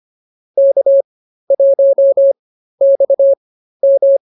Morse code, heard as a keyed tone, about 24 wpm, spaced as for 13 wpm, 550 Hz. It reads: K1XM